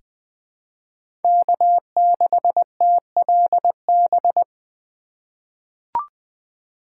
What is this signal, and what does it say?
Morse code, heard as a keyed tone, about 21 wpm, 700 Hz. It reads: K6TLB
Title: USA Call Signs - 4x Character Spacing - 20wpm